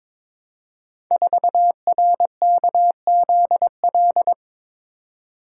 Morse code, heard as a keyed tone, about 22 wpm, 700 Hz. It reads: 4RKZL
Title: Code Groups - 5 Letter:Numbers - 22wpm